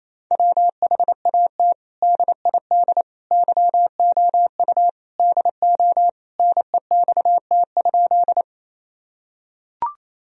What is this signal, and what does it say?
Morse code, heard as a keyed tone, about 28 wpm, 700 Hz. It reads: WHAT DID YOU DO NEXT?
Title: Sentences from Top 1000 Words - Review Part 1 - 28wpm